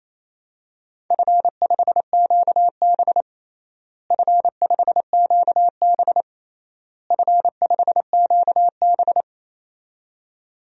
Morse code, heard as a keyed tone, about 28 wpm, 700 Hz. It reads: F5QB F5QB F5QB
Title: DXCC-100 Call Signs - 4 Characters - Repeated 3x - 28wpm